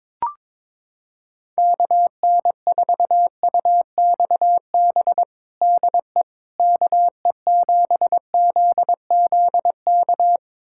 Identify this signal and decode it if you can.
Morse code, heard as a keyed tone, about 22 wpm, 700 Hz. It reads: KN4UXB DE KE7ZZK
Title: USA Callsign DE Callsign - 6 Characters - 22wpm